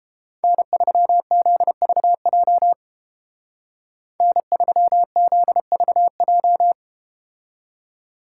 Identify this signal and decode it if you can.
Morse code, heard as a keyed tone, about 31 wpm, 700 Hz. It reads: N3ZVJ N3ZVJ